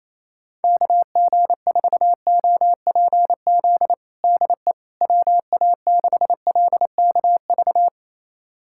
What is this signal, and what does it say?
Morse code, heard as a keyed tone, about 28 wpm, 700 Hz. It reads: KG4OPZ DE WA6LKV
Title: USA Callsign DE Callsign - 6 Characters - 28wpm